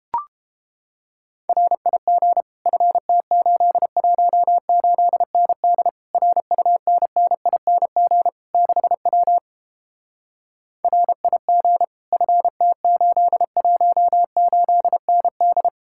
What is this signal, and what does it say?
Morse code, heard as a keyed tone, about 29 wpm, 700 Hz. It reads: RIG FT818ND RUNNING 6W RIG FT818ND